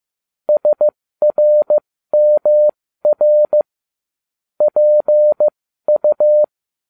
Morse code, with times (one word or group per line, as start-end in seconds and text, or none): 0.5-3.6 SRMR
4.6-6.4 PU